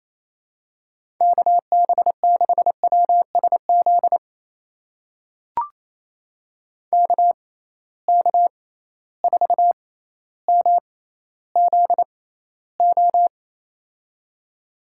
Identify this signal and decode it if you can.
Morse code, heard as a keyed tone, about 28 wpm, spaced as for 14 wpm, 700 Hz. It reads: KB6WSZ KK4MZO
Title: USA Call Signs - 6 Characters - 6x Character Spacing - 28wpm